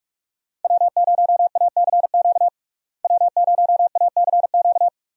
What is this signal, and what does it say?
Morse code, heard as a keyed tone, about 46 wpm, 700 Hz. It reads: W0ACQ W0ACQ
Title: USA Call Signs - 5 Characters - Repeated 3x - 45wpm